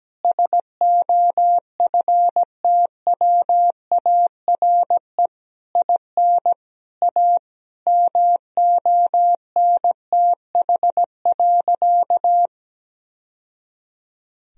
Morse code, heard as a keyed tone, about 17 wpm, 700 Hz. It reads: SOFTWARE IN A MONTH.
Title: Sentences from Top 1000 Words - Review Part 3 - 17wpm